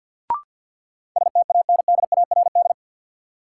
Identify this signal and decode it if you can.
Morse code, heard as a keyed tone, about 53 wpm, 700 Hz. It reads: STANDARD
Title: Sentences from Top 1000 Words - Review Part 6 - New Word Drill - 50wpm